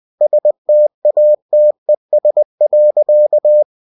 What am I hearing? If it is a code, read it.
Morse code, heard as a keyed tone, about 20 wpm, 600 Hz. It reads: STATES.